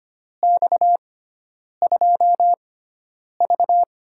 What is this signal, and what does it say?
Morse code, heard as a keyed tone, about 25 wpm, spaced as for 9 wpm, 700 Hz. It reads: X2V
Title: USA Call Signs - 4 Characters - 6x Character Spacing - 25wpm